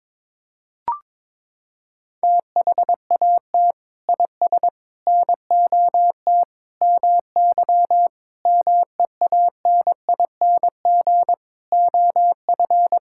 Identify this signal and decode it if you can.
Morse code, heard as a keyed tone, about 22 wpm, 700 Hz. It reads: THAT IS NOT MY MEANING OF
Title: Sentences from Top 1000 Words - Review Part 6 - 22wpm